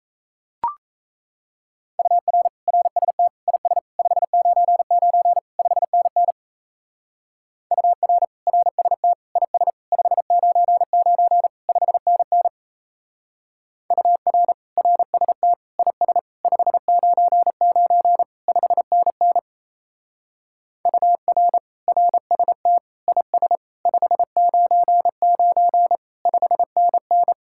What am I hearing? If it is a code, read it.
Morse code, heard as a keyed tone, about 34 wpm, 700 Hz. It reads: UR RST IS 599 5NN UR RST IS 599 5NN UR RST IS 599 5NN UR RST IS 599 5NN
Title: Speed-Racing Common QSO Phrases Encore - 28wpm